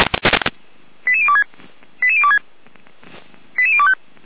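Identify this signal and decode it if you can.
Five-tone selective call (ZVEI), heard as quick runs of five tones, tones of about 70 ms. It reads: 88026 88026 88025
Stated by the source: Received on a software-defined radio (SDR); ZVEI five-tone selective call 88213.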